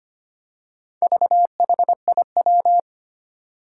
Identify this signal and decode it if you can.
Morse code, heard as a keyed tone, about 25 wpm, 700 Hz. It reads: VHIW
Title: Four Challenging Letter-Number Combinations 25wpm